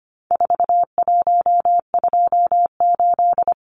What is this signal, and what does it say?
Morse code, heard as a keyed tone, about 25 wpm, 700 Hz. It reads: 4128
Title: Numbers - 4-Digit 25wpm